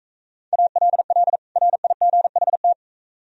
Morse code, heard as a keyed tone, about 42 wpm, 700 Hz. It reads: ALL RIGHT